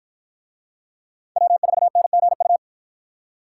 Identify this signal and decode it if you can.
Morse code, heard as a keyed tone, about 48 wpm, 700 Hz. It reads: W4NGU